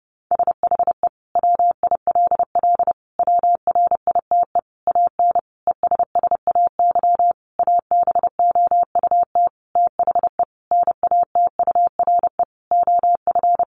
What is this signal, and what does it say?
Morse code, heard as a keyed tone, about 30 wpm, 700 Hz. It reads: SHE WILL WRITE AN ESSAY ABOUT THE NATURE OF
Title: Sentences from Top 1500 Words 30wpm